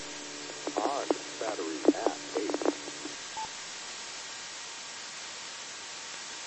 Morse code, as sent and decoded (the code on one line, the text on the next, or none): none